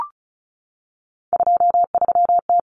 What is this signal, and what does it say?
Morse code, heard as a keyed tone, about 34 wpm, 700 Hz. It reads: 23T
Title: Common Cut Numbers - 3-Digit 35wpm